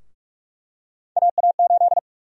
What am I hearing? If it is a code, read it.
Morse code, heard as a keyed tone, about 47 wpm, 700 Hz. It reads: AA8
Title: USA Call Sign Prefixes - Repeated 2x - 45wpm